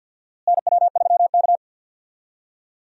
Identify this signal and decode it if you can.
Morse code, heard as a keyed tone, about 53 wpm, 700 Hz. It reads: NW3K
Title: USA Call Signs - 4 Characters - Repeated 2x - 50wpm